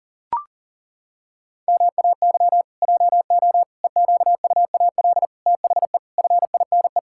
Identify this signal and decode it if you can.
Morse code, heard as a keyed tone, about 40 wpm, 700 Hz. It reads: MAY JO EQUAL THE FINE
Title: Pangram Challenge v2 - 40wpm